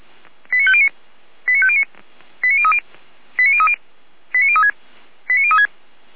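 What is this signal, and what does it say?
Five-tone selective call (ZVEI), heard as quick runs of five tones, tones of about 70 ms. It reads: 89509 89509 89030 89030 89036 89036
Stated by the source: ZVEI five-tone selective call 89140 received with an SDR, from the Feuerwehr Orlamünde fire brigade audio server.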